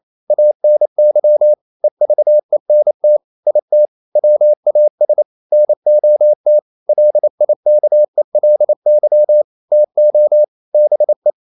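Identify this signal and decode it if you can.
Morse code, heard as a keyed tone, about 28 wpm, 600 Hz. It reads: ANY EVENT IT WAS NOT LIKELY TO BE
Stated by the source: Morse code practice at 28 wpm, text of Ken Follett's novel Never in English, from CW med Gnister.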